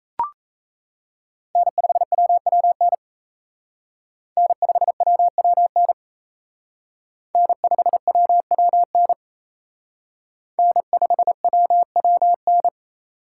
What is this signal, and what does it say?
Morse code, heard as a keyed tone, about 35 wpm, 700 Hz. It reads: N5WWN N5WWN N5WWN N5WWN